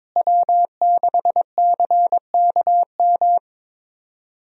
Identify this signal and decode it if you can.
Morse code, heard as a keyed tone, about 22 wpm, 700 Hz. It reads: W6CKM
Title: USA Call Signs - 5 Characters - Repeated 3x - 22wpm